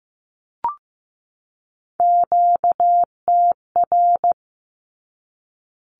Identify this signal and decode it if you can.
Morse code, heard as a keyed tone, about 15 wpm, 700 Hz. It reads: QTR
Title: Q-Codes 15wpm